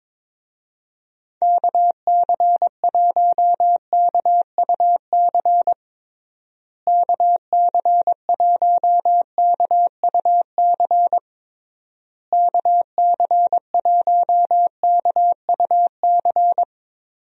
Morse code, heard as a keyed tone, about 22 wpm, 700 Hz. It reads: KC1KUC KC1KUC KC1KUC
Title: USA Call Signs - 6 Characters - Repeated 3x - 22wpm